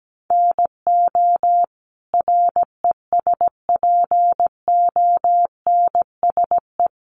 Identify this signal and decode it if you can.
Morse code, heard as a keyed tone, about 17 wpm, 700 Hz. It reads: NO RESPONSE
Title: Sentences from Top 1400 Words 17wpm